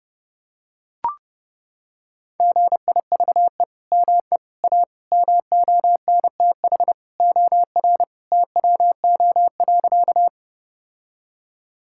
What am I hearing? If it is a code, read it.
Morse code, heard as a keyed tone, about 32 wpm, 700 Hz. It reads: GIVE ME A MONTH OR TWO.
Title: Sentences from Top 1000 Words - Review Part 5 - 30wpm